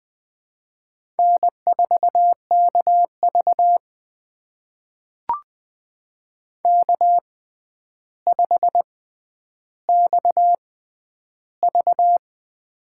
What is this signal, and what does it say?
Morse code, heard as a keyed tone, about 20 wpm, spaced as for 10 wpm, 700 Hz. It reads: N4KV K5XV
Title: USA Call Signs - 4 Characters - 6x Character Spacing - 20wpm